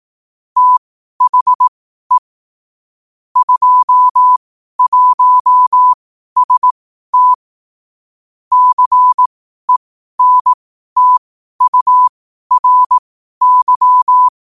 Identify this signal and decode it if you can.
Morse code, heard as a keyed tone, about 18 wpm, spaced as for 12 wpm, 1000 Hz. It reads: THE 21ST CENTURY
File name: qotdfiles / 12wpm